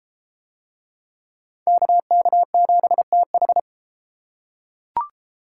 Morse code, heard as a keyed tone, about 31 wpm, 700 Hz. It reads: KK7TH